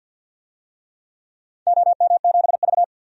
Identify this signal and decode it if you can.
Morse code, heard as a keyed tone, about 53 wpm, 700 Hz. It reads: OM7V
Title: DXCC-100 Call Signs - 4 Characters - 50wpm